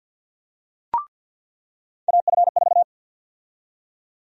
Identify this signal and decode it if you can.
Morse code, heard as a keyed tone, about 47 wpm, 700 Hz. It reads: AF4
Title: USA Call Sign Prefixes - 50wpm